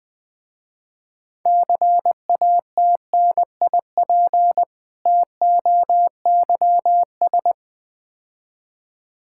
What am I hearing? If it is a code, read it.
Morse code, heard as a keyed tone, about 20 wpm, 700 Hz. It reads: CATNIP TOYS
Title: Difficult Words with Hints - 20wpm